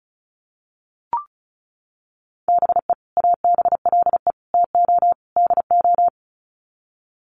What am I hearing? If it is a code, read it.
Morse code, heard as a keyed tone, about 34 wpm, 700 Hz. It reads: BE ABLE TO DO
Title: Sets of 4 Words 35wpm